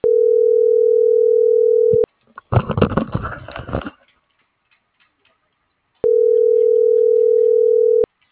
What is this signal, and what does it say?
Ringback tone (North America), 440 + 480 Hz.